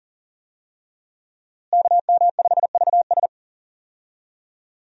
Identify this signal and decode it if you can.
Morse code, heard as a keyed tone, about 40 wpm, 700 Hz. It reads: KM5VS
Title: USA Call Signs - 5 Characters - Repeated 3x - 40wpm